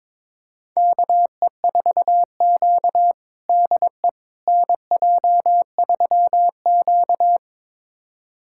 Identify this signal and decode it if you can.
Morse code, heard as a keyed tone, about 22 wpm, 700 Hz. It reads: KE4Q DE NJ3Q
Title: USA Callsign DE Callsign - 4 Characters - 22wpm